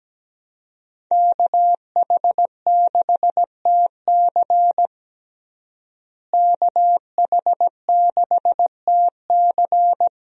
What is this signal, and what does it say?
Morse code, heard as a keyed tone, about 17 wpm, 700 Hz. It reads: KH6TC KH6TC
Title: DXCC-100 Call Signs - 5 Characters - Repeated 2x - 17wpm